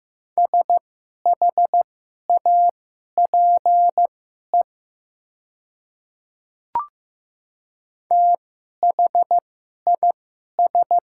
Morse code, heard as a keyed tone, about 15 wpm, spaced as for 11 wpm, 700 Hz. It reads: SHAPE THIS
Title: 2x Character Spacing - Top 500 Words - 15wpm